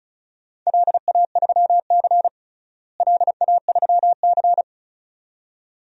Morse code, heard as a keyed tone, about 36 wpm, 700 Hz. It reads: LA3C LA3C
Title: DXCC-100 Call Signs - 4 Characters - Repeated 2x - 35wpm